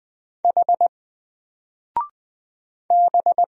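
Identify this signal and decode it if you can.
Morse code, heard as a keyed tone, about 20 wpm, 700 Hz. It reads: H B
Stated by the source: Single letter and number Morse drill at 20 wpm, from N4LCD.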